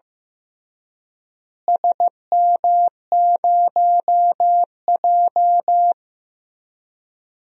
Morse code, heard as a keyed tone, about 15 wpm, 700 Hz. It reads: SM0J
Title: DXCC-100 Call Signs - 4 Characters - Repeated 3x - 15wpm